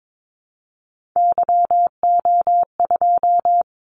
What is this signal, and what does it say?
Morse code, heard as a keyed tone, about 22 wpm, 700 Hz. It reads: YO2
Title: Callsign Prefixes for Top 25 DXCC - 22wpm